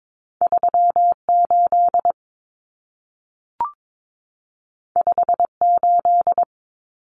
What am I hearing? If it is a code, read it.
Morse code, heard as a keyed tone, about 22 wpm, 700 Hz. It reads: 38 58
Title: Common Cut Numbers - 2-Digit 22wpm